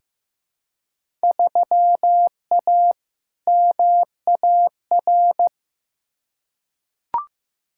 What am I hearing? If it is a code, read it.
Morse code, heard as a keyed tone, about 15 wpm, 700 Hz. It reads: 3A MAR
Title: Field Day Exchanges Encore - 15wpm